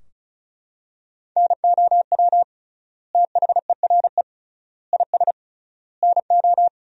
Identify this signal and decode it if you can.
Morse code, heard as a keyed tone, about 34 wpm, 700 Hz. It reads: NOW THERE IS NO
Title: 3x Word-Spacing - Sentences from Top 100 Words - 35wpm